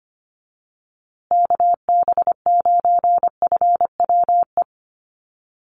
Morse code, heard as a keyed tone, about 25 wpm, 700 Hz. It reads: KB9FWE
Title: USA Call Signs Repeated 2x 25wpm